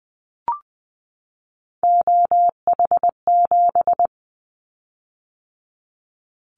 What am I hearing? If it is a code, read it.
Morse code, heard as a keyed tone, about 20 wpm, 700 Hz. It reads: OH7
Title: Callsign Prefixes for Top 25 DXCC - 20wpm